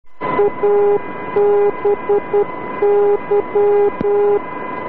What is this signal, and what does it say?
Morse code, heard as a keyed tone, about 10 wpm, 410 Hz, with noise. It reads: ABY